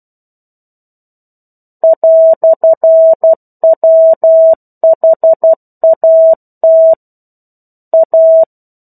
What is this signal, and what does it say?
Morse code, heard as a keyed tone, about 12 wpm, 650 Hz. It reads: "WHAT A
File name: 01-Chapter-01 12 WPM